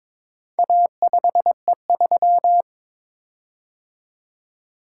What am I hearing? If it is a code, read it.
Morse code, heard as a keyed tone, about 22 wpm, 690 Hz. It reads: A5E3
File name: practice / 22wpm / Four Letter-Number Combinations - 22wpm